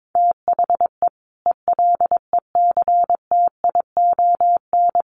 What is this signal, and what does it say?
Morse code, heard as a keyed tone, about 22 wpm, 700 Hz. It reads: THE ELECTION
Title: Sentences from Top 1500 Words 22wpm